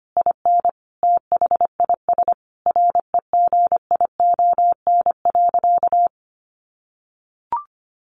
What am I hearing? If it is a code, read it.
Morse code, heard as a keyed tone, about 25 wpm, 700 Hz. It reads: IN THIS REGION.
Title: Sentences from Top 1500 Words 25wpm